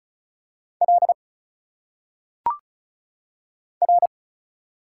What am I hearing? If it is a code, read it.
Morse code, heard as a keyed tone, about 34 wpm, 700 Hz. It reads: L R
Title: Single Letters - Phonetic Pronunciation 35wpm